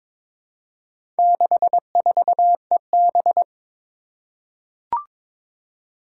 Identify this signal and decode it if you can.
Morse code, heard as a keyed tone, about 22 wpm, 700 Hz. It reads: 64EB
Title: Four Challenging Letter-Number Combinations 22wpm